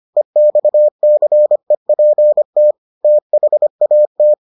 Morse code, heard as a keyed tone, about 25 wpm, 600 Hz. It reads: EXCEPT THAT